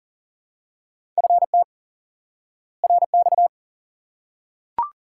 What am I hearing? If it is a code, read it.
Morse code, heard as a keyed tone, about 40 wpm, 700 Hz. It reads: FT RX
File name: IQE Course 3 - Lesson 89 - YAGI - Two Elements - 40wpm